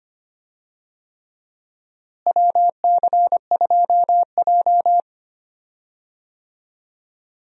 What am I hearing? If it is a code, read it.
Morse code, heard as a keyed tone, about 25 wpm, 700 Hz. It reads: WC2J
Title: USA Call Signs - 4 Characters - Repeated 3x - 25wpm